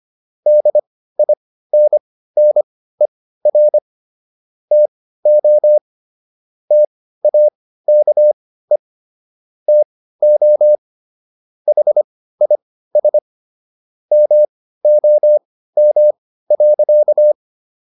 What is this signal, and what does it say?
Morse code, heard as a keyed tone, about 25 wpm, spaced as for 15 wpm, 600 Hz. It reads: DINNER TO TAKE TO HIS MOM.